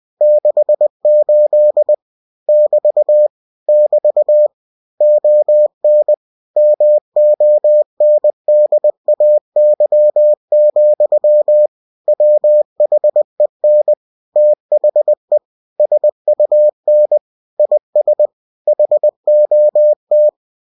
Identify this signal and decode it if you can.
Morse code, heard as a keyed tone, about 20 wpm, 600 Hz. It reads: 68 <BT> <BT> ON MONDAY, WHEN THE SUN IS HOT
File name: cw-books / winnie-the-pooh / winnie-20wpm-page-0068